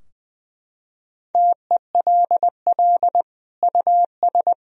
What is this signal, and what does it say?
Morse code, heard as a keyed tone, about 20 wpm, 700 Hz. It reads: TELL US
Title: Sentences from Top 1000 Words - Review Part 3 - 20wpm